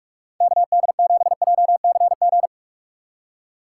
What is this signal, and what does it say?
Morse code, heard as a keyed tone, about 42 wpm, 700 Hz. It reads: KD7JCG
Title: USA Call Signs - 6x Character Spacing - 45wpm